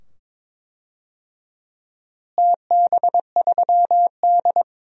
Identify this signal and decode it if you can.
Morse code, heard as a keyed tone, about 22 wpm, 700 Hz. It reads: TB3D
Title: DXCC-100 Call Signs - 4 Characters - Repeated 2x - 22wpm